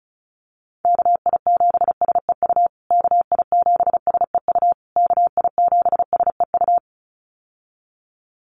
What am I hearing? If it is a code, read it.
Morse code, heard as a keyed tone, about 34 wpm, 700 Hz. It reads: KI7SEU KI7SEU KI7SEU
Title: USA Call Signs Repeated 3x 35wpm